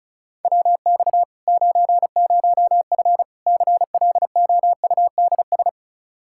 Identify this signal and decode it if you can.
Morse code, heard as a keyed tone, about 36 wpm, 700 Hz. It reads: WX 90F CLOUDS
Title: Practice QSO - Exchange 2 - Info - 35wpm